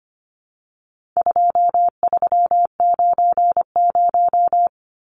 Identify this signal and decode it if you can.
Morse code, heard as a keyed tone, about 25 wpm, 700 Hz. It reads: 2390
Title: Numbers - 4-Digit 25wpm